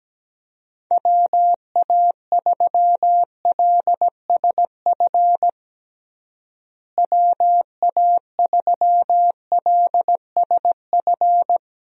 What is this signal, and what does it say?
Morse code, heard as a keyed tone, about 17 wpm, 700 Hz. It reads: WA3LSF WA3LSF
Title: USA Call Signs - 6 Characters - Repeated 2x - 17wpm